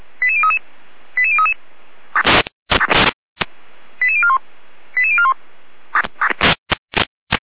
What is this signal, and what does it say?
Five-tone selective call (ZVEI), heard as quick runs of five tones, tones of about 70 ms. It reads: 88033 88033 88041 88041
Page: SDR/ZVEI